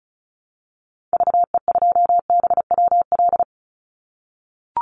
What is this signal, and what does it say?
Morse code, heard as a keyed tone, about 34 wpm, 700 Hz. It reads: VE2BWL
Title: Call Signs - 35wpm